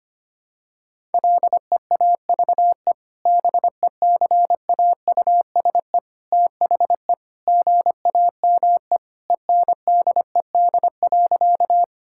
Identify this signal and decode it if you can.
Morse code, heard as a keyed tone, about 25 wpm, 700 Hz. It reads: LEAVE BECAUSE THE GAME ENDED.
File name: practice / 25wpm / Sentences with 2-Word Phrases - 25wpm